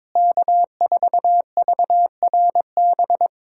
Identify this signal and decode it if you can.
Morse code, heard as a keyed tone, about 22 wpm, 700 Hz. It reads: K4VRB
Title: USA Callsign DE Callsign 22wpm